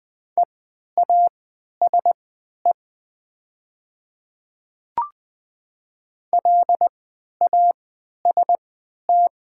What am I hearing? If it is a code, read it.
Morse code, heard as a keyed tone, about 20 wpm, spaced as for 11 wpm, 700 Hz. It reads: EASE LAST
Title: 3x Character Spacing - Top 300 Words - 20wpm